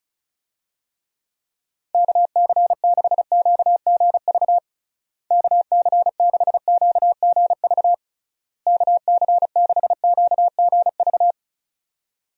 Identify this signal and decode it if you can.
Morse code, heard as a keyed tone, about 34 wpm, 700 Hz. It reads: KC6QGV KC6QGV KC6QGV
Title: USA Call Signs - 6 Characters - Repeated 3x - 35wpm